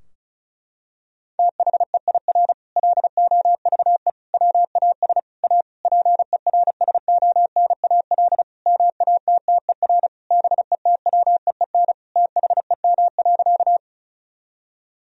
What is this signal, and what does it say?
Morse code, heard as a keyed tone, about 36 wpm, 700 Hz. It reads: THEIR LOVE WAS A PERSONAL MATTER BETWEEN THEM.
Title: Sentences from Top 1000 Words - Review Part 4 - 35wpm